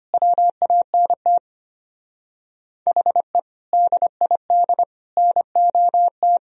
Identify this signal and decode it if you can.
Morse code, heard as a keyed tone, about 33 wpm, 700 Hz. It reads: WANT HE DID NOT